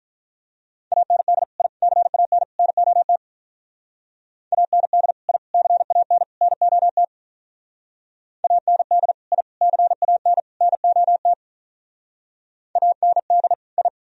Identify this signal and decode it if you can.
Morse code, heard as a keyed tone, about 45 wpm, 700 Hz. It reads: AND I CAN NOT AND I CAN NOT AND I CAN NOT AND I